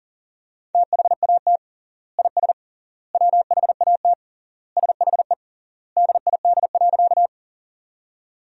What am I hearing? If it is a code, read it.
Morse code, heard as a keyed tone, about 40 wpm, 700 Hz. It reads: THAT IS WHAT SHE DID.
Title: 3x Word-Spacing - Sentences from Top 100 Words - 40wpm